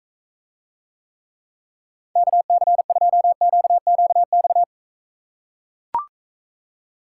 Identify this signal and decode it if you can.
Morse code, heard as a keyed tone, about 41 wpm, 700 Hz. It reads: KC2QQX